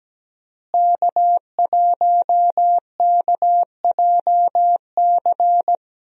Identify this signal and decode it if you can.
Morse code, heard as a keyed tone, about 17 wpm, 700 Hz. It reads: K1KJC
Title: USA Call Signs - 5 Characters - Repeated 3x - 17wpm